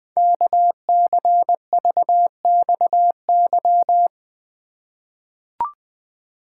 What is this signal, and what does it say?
Morse code, heard as a keyed tone, about 20 wpm, 700 Hz. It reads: KCVXY
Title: Code Groups - 5 Letters - 20wpm